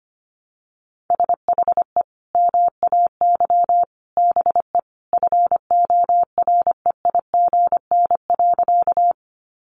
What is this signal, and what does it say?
Morse code, heard as a keyed tone, about 25 wpm, 700 Hz. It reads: SHE MAY BE FOREIGN.
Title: Sentences from Top 1400 Words 25wpm